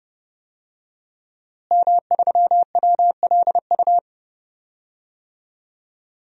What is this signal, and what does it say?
Morse code, heard as a keyed tone, about 30 wpm, 700 Hz. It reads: M3WLU
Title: DXCC-100 Call Signs - 5 Characters - Repeated 3x - 30wpm